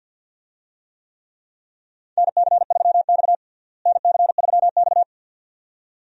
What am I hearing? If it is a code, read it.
Morse code, heard as a keyed tone, about 51 wpm, 700 Hz. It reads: NC3X NC3X
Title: USA Call Signs - 4 Characters - Repeated 2x - 50wpm